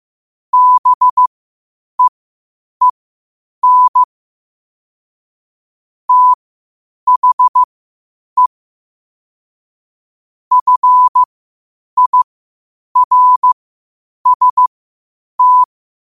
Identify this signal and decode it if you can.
Morse code, heard as a keyed tone, about 15 wpm, spaced as for 8 wpm, 1000 Hz. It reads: BEEN THE FIRST
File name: qotdfiles / 8wpm